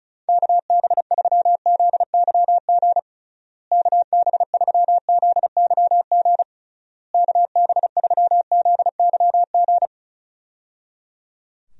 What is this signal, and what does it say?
Morse code, heard as a keyed tone, about 36 wpm, 700 Hz. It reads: KB3ZYG KB3ZYG KB3ZYG
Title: USA Call Signs - 6 Characters - Repeated 3x - 35wpm